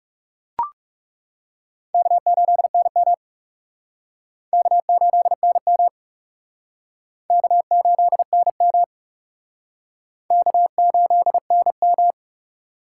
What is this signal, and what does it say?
Morse code, heard as a keyed tone, about 35 wpm, 700 Hz. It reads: K8NM K8NM K8NM K8NM
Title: Speed Racing - USA 1x2 Call Signs 30wpm